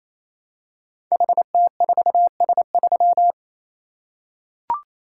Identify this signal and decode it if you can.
Morse code, heard as a keyed tone, about 27 wpm, 700 Hz. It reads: HT4S3
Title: Code Groups - 5 Letter:Numbers - 28wpm